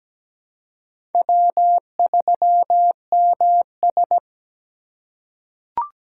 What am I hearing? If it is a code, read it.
Morse code, heard as a keyed tone, about 17 wpm, 700 Hz. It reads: W3MS
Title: K1USN SST QSO Practice - 17wpm